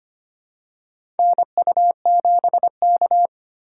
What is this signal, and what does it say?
Morse code, heard as a keyed tone, about 25 wpm, 700 Hz. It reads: NU7K
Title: USA Call Signs - 4 Characters - 4x Character Spacing - 25wpm